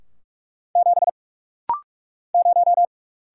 Morse code, heard as a keyed tone, about 44 wpm, 700 Hz. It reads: 7 0
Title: Single Letter-Number - Rapid-Fire 45wpm